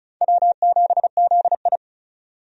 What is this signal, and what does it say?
Morse code, heard as a keyed tone, about 36 wpm, 700 Hz. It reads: W7ZI
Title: USA Call Signs - 4 Characters - 35wpm